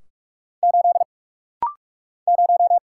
Morse code, heard as a keyed tone, about 46 wpm, 700 Hz. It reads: 8 0
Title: Single Letter-Number - Rapid-Fire 45wpm